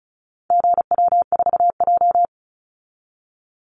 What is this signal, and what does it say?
Morse code, heard as a keyed tone, about 34 wpm, 700 Hz. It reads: GW4J
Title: Call Signs Repeated 2x - 35wpm